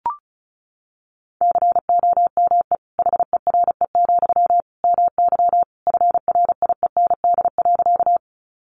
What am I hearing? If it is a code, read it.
Morse code, heard as a keyed tone, about 34 wpm, 700 Hz. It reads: COME HERE, MY FRIEND.
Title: Sentences from Top 300 Words 35wpm